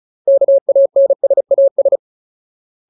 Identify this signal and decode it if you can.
Morse code, heard as a keyed tone, about 35 wpm, 550 Hz. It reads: KANSAS
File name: Ultimate Word List / 35WPM Ultimate